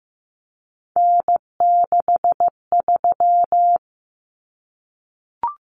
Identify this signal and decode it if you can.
Morse code, heard as a keyed tone, about 15 wpm, 700 Hz. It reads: N63
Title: Common Cut Numbers - 3-Digit 15wpm